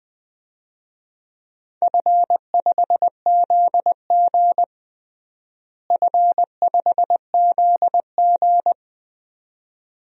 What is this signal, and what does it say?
Morse code, heard as a keyed tone, about 21 wpm, 700 Hz. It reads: F5ZG F5ZG
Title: DXCC-100 Call Signs - 4 Characters - Repeated 2x - 20wpm